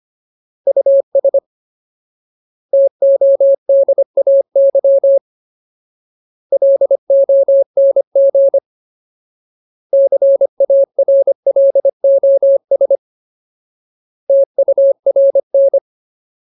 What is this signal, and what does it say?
Morse code, heard as a keyed tone, about 25 wpm, 550 Hz. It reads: US TODAY LONG CARLOS TURN
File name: Ultimate Word List / 25WPM Ultimate